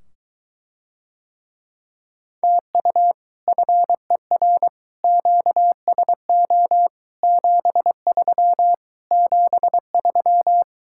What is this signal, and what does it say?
Morse code, heard as a keyed tone, about 23 wpm, 700 Hz. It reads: TU FER QSO 73 73